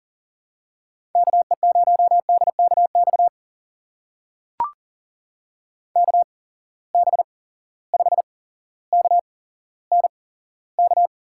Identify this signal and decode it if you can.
Morse code, heard as a keyed tone, about 40 wpm, spaced as for 17 wpm, 700 Hz. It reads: KE0DKX KB5KNK
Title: USA Call Signs - 6 Characters - 8x Character Spacing - 40wpm